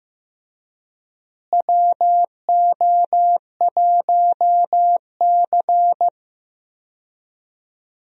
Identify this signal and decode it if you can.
Morse code, heard as a keyed tone, about 16 wpm, 700 Hz. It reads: WO1C
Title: USA Call Signs - 4 Characters - Repeated 3x - 15wpm